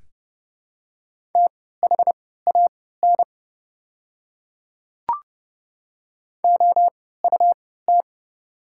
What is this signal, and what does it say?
Morse code, heard as a keyed tone, about 30 wpm, spaced as for 17 wpm, 700 Hz. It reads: THAN OUT
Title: 3x Character Spacing - Top 100 Words - 30wpm